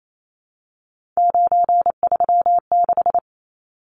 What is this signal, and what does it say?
Morse code, heard as a keyed tone, about 28 wpm, 700 Hz. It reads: 936
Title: Numbers - 3-Digit 28wpm